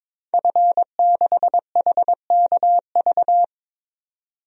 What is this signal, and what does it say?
Morse code, heard as a keyed tone, about 23 wpm, 700 Hz. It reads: F6HKV
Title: DXCC-100 Call Signs - 5 Characters - 22wpm